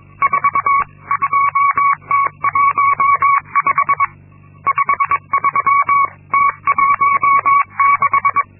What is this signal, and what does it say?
Morse code, heard as a keyed tone, about 22 wpm, 1120 Hz. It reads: 42T15 53T16
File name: Linksmakalnis-2